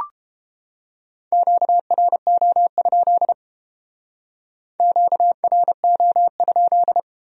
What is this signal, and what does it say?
Morse code, heard as a keyed tone, about 31 wpm, 700 Hz. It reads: QRO? QRO?